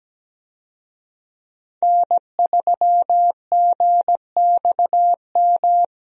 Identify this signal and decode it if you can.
Morse code, heard as a keyed tone, about 17 wpm, 700 Hz. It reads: N3GXM
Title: USA Call Signs - 5 Characters - Repeated 2x - 17wpm